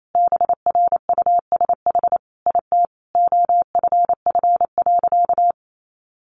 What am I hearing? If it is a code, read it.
Morse code, heard as a keyed tone, about 28 wpm, 700 Hz. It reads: BRUSH IT OFF.
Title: Sentences from Top 1100 Words 28wpm